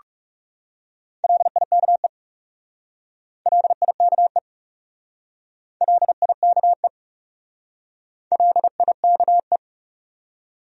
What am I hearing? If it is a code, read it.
Morse code, heard as a keyed tone, about 38 wpm, 700 Hz. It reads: LIKE LIKE LIKE LIKE